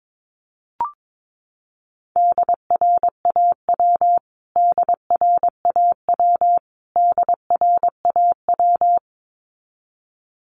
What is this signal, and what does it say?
Morse code, heard as a keyed tone, about 22 wpm, 700 Hz. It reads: DRAW DRAW DRAW
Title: Top 500 Words Repeated 22wpm